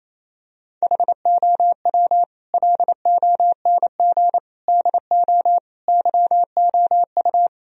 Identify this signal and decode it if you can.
Morse code, heard as a keyed tone, about 28 wpm, 700 Hz. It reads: HOW LONG DO YOU
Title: Sentences from Top 1000 Words - Review Part 2 - 28wpm